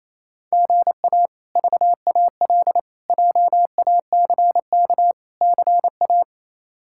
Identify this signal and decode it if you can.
Morse code, heard as a keyed tone, about 28 wpm, 700 Hz. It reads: GA VAL JACK CA
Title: K1USN SST QSO Practice - 28wpm